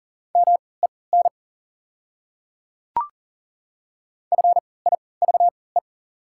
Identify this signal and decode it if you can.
Morse code, heard as a keyed tone, about 40 wpm, spaced as for 23 wpm, 690 Hz. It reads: MEN FIVE
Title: 3x Character Spacing - Top 500 Words - 40wpm